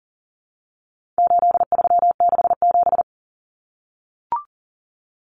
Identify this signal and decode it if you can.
Morse code, heard as a keyed tone, about 40 wpm, 700 Hz. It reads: 8367
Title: Common Cut Numbers - 4-Digit 40wpm